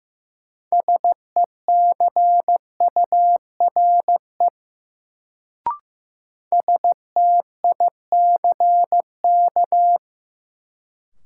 Morse code, heard as a keyed tone, about 15 wpm, 700 Hz. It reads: SECURE STICK
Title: Sentences from Top 1200 Words - New Word Drill - 15wpm